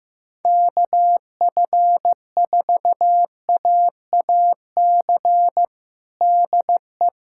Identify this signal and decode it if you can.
Morse code, heard as a keyed tone, about 16 wpm, 700 Hz. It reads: KF4AAC DE
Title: USA Callsign DE Callsign 15wpm